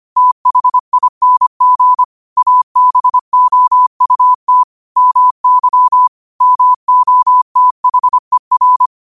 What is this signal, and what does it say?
Morse code, heard as a keyed tone, about 25 wpm, 1000 Hz. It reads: THING ABOUT MY MOTHER